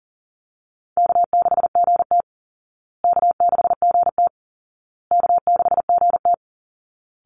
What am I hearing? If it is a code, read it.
Morse code, heard as a keyed tone, about 40 wpm, 700 Hz. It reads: K6GT K6GT K6GT
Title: Call Signs Repeated 2x - 40wpm